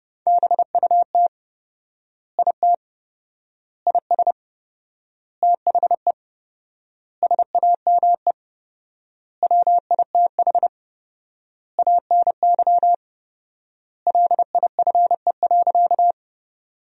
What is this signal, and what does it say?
Morse code, heard as a keyed tone, about 30 wpm, 700 Hz. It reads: BUT IT IS THE SAME WITH ANY LIFE.
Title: 4x Word-Spacing - Sentences from Top 300 Words - 30wpm